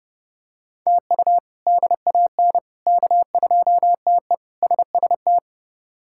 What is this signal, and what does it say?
Morse code, heard as a keyed tone, about 30 wpm, 700 Hz. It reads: TU DAN K2TE SST
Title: K1USN SST QSO Practice - 30wpm